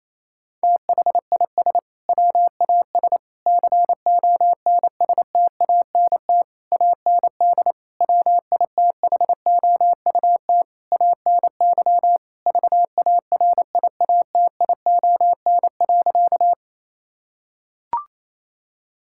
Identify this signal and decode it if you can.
Morse code, heard as a keyed tone, about 28 wpm, 700 Hz. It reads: THIS WAS CONSTANT AND WITHOUT ANY VARIATION.
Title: Sentences from Top 1600 Words 28wpm